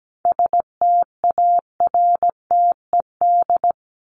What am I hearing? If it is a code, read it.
Morse code, heard as a keyed tone, about 17 wpm, 700 Hz. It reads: STARTED
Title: ED Suffix - Top 100 Words with Base Words 17wpm